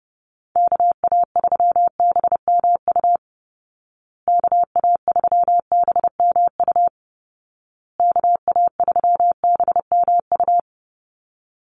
Morse code, heard as a keyed tone, about 30 wpm, 700 Hz. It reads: KA3BMU KA3BMU KA3BMU
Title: Call Signs Repeated 2x - 30wpm